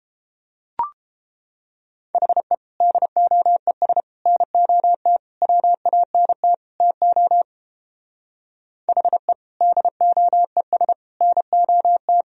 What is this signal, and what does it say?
Morse code, heard as a keyed tone, about 31 wpm, 700 Hz. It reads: HE DOES NOT WANT TO HE DOES NOT